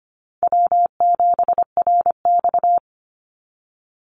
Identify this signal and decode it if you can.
Morse code, heard as a keyed tone, about 25 wpm, 700 Hz. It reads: W7RX
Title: Call Signs Repeated 2x - 25wpm